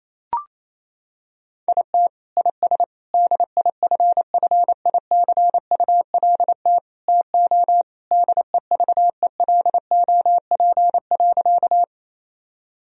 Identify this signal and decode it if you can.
Morse code, heard as a keyed tone, about 28 wpm, 700 Hz. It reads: IT IS DIFFICULT TO DEVELOP.
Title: Sentences from Top 1000 Words - Review Part 2 - 28wpm